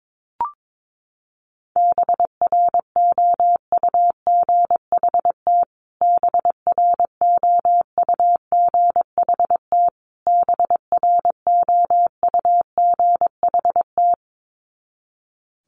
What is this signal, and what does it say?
Morse code, heard as a keyed tone, about 22 wpm, 700 Hz. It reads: BROUGHT BROUGHT BROUGHT
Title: Top 500 Words Repeated 22wpm